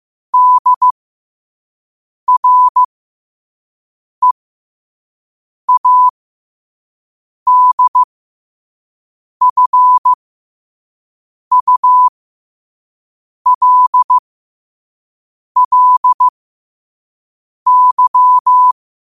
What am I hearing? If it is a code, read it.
Morse code, heard as a keyed tone, about 15 wpm, spaced as for 5 wpm, 1000 Hz. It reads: DREADFULLY
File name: qotdfiles / 5wpm